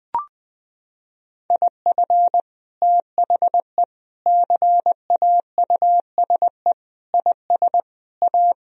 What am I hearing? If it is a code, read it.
Morse code, heard as a keyed tone, about 20 wpm, 700 Hz. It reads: IF THE CAUSE IS A
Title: Sentences from Top 1000 Words - Review Part 1 - 20wpm